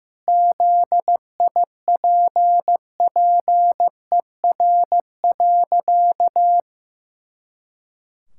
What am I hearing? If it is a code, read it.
Morse code, heard as a keyed tone, about 15 wpm, 700 Hz. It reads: ZIPPER.
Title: Pangram Challenge v2 - 15wpm